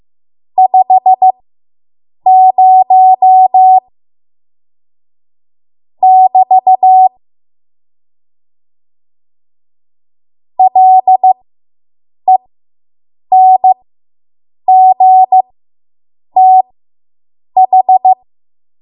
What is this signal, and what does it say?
Morse code, heard as a keyed tone, about 15 wpm, spaced as for 7 wpm, 750 Hz. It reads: 50 <BT> LENGTH